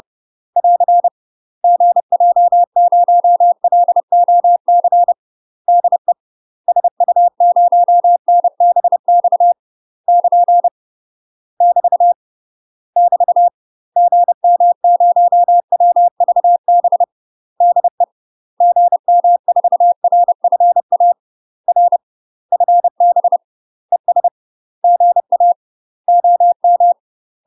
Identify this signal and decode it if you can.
Morse code, heard as a keyed tone, about 30 wpm, 700 Hz. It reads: <AR> GJ0LOC DE SU0NBX <KN> <BT> <BT> GM0WVB DE GM4RFA R FB ES GA OM
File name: UK MASTER QSOs TRAINING / 30WPM / UK MASTER QSOs 0014